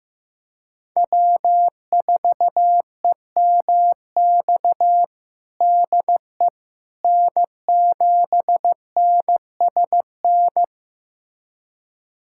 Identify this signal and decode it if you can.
Morse code, heard as a keyed tone, about 15 wpm, 700 Hz. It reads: W4EMX DE N7NSN
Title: USA Callsign DE Callsign - 5 Characters - 15wpm